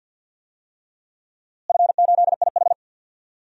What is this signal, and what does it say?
Morse code, heard as a keyed tone, about 51 wpm, 700 Hz. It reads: F8IH
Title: DXCC-100 Call Signs - 4 Characters - Repeated 3x - 50wpm